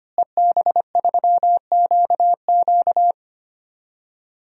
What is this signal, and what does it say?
Morse code, heard as a keyed tone, about 25 wpm, 700 Hz. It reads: EB3QQ
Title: DXCC-100 Call Signs - 5 Characters - Repeated 2x - 25wpm